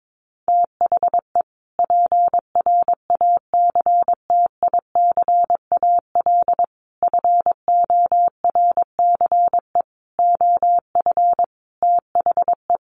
Morse code, heard as a keyed tone, about 22 wpm, 700 Hz. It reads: THE PRACTICAL FORCE OF THE
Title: Sentences from Top 1400 Words 22wpm